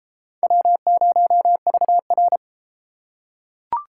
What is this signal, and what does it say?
Morse code, heard as a keyed tone, about 31 wpm, 700 Hz. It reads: W0VR